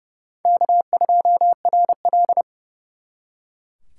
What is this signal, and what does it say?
Morse code, heard as a keyed tone, about 34 wpm, 700 Hz. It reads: K2RL
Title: Speed Racing - USA 1x2 Call Signs 30wpm